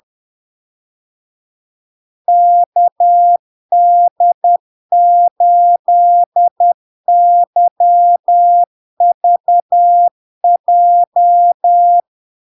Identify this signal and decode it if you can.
Morse code, heard as a keyed tone, about 10 wpm, 700 Hz. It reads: KD8YVJ
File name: LARK-FD-2022-Log-in-CW-10wpm-700-Hz